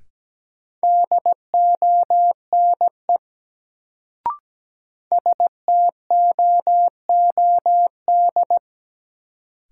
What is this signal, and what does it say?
Morse code, heard as a keyed tone, about 17 wpm, 700 Hz. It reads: DONE STOOD
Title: Sentences from Top 500 Words - New Word Drill - 17wpm